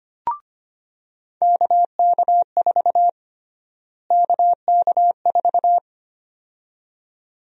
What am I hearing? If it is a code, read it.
Morse code, heard as a keyed tone, about 25 wpm, 700 Hz. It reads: KK4 KK4
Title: USA Call Sign Prefixes - Repeated 2x - 25wpm